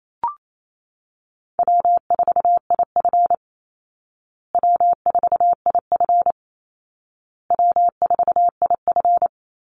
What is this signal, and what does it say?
Morse code, heard as a keyed tone, about 28 wpm, 700 Hz. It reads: W4IF W4IF W4IF
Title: Call Signs Repeated 2x - 28wpm